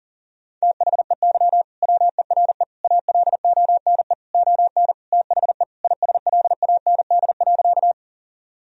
Morse code, heard as a keyed tone, about 40 wpm, 700 Hz. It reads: THEY WERE ALONE ON THE ISLAND.
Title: Sentences from Top 1000 Words - Review Part 3 - 40wpm